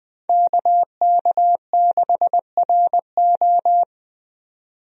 Morse code, heard as a keyed tone, about 20 wpm, 700 Hz. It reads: KK6RO
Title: USA Call Signs - 5 Characters - 20wpm